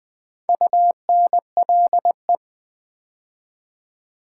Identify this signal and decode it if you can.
Morse code, heard as a keyed tone, about 20 wpm, 700 Hz. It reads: UNLE
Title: Build A Word - 2 Letters At A Time - 6-Letter Words - 20wpm